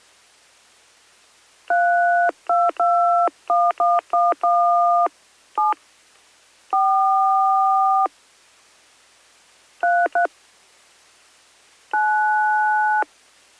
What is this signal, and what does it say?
DTMF, heard as telephone touch tones, tones of about 240 ms. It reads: 3221111 7 4 33 9